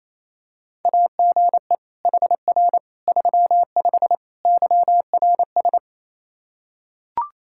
Morse code, heard as a keyed tone, about 28 wpm, 700 Hz. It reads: AGE HR 35 YRS
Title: Practice QSO - Exchange 3 - More Info - 28wpm